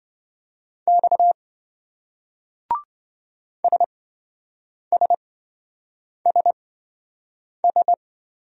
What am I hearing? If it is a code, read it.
Morse code, heard as a keyed tone, about 26 wpm, spaced as for 7 wpm, 700 Hz. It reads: X SSSS